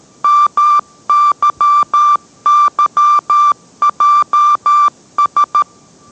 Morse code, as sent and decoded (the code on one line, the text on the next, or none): -- -.-- -.-- .--- ...
MYYJS